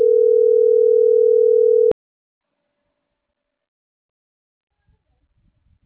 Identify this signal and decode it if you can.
Ringback tone (North America), 440 + 480 Hz.